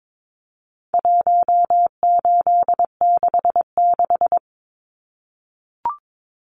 Morse code, heard as a keyed tone, about 22 wpm, 700 Hz. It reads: 1866
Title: Numbers - 4-Digit 22wpm